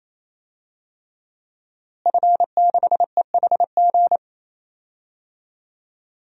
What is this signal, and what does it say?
Morse code, heard as a keyed tone, about 28 wpm, 700 Hz. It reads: F6EHG
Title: DXCC-100 Call Signs - 5 Characters - Repeated 3x - 28wpm